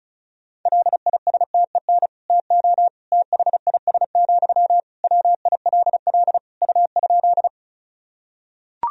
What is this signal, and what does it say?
Morse code, heard as a keyed tone, about 34 wpm, 700 Hz. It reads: LISTEN TO THIS, WILL U?
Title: Sentences from Top 400 Words and Abbreviations 35wpm